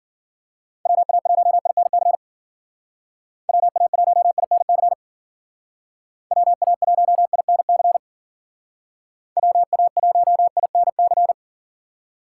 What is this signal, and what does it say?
Morse code, heard as a keyed tone, about 52 wpm, 700 Hz. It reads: WA1INC WA1INC WA1INC WA1INC